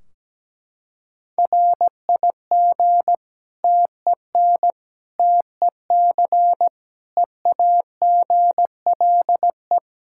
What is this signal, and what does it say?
Morse code, heard as a keyed tone, about 17 wpm, 700 Hz. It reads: RIG TEN TEC EAGLE
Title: Practice QSO - Exchange 2 - Info - 17wpm